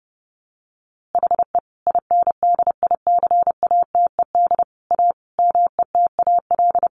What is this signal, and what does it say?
Morse code, heard as a keyed tone, about 30 wpm, 700 Hz. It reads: HE INDICATED A METAL
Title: A Sound Of Thunder 30wpm